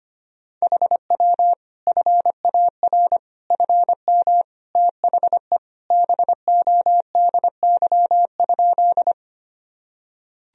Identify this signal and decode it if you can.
Morse code, heard as a keyed tone, about 25 wpm, 700 Hz. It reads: HW FAR FM THE BODY?
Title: Sentences from Top 400 Words and Abbreviations 25wpm